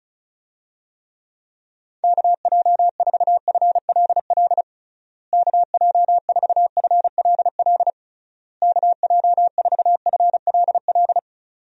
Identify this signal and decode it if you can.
Morse code, heard as a keyed tone, about 36 wpm, 700 Hz. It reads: KJ4FLL KJ4FLL KJ4FLL
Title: USA Call Signs - 6 Characters - Repeated 3x - 35wpm